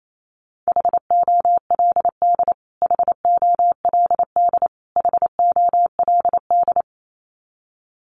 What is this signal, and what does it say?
Morse code, heard as a keyed tone, about 28 wpm, 700 Hz. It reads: HOLD HOLD HOLD
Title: Top 500 Words Repeated 28wpm